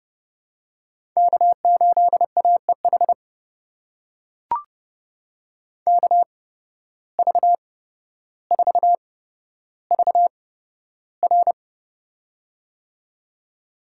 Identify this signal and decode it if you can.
Morse code, heard as a keyed tone, about 30 wpm, spaced as for 13 wpm, 700 Hz. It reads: K8AEH KV4VR
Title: USA Call Signs - 8x Character Spacing - 30wpm